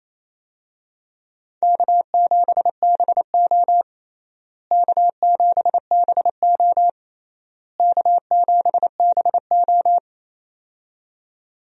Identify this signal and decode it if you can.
Morse code, heard as a keyed tone, about 28 wpm, 700 Hz. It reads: K7BO K7BO K7BO
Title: USA Call Signs - 4 Characters - Repeated 3x - 28wpm